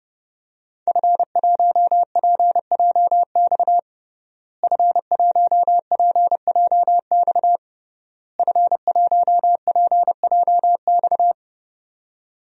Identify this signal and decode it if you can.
Morse code, heard as a keyed tone, about 30 wpm, 700 Hz. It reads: F1PJX F1PJX F1PJX
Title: DXCC-100 Call Signs - 5 Characters - Repeated 3x - 30wpm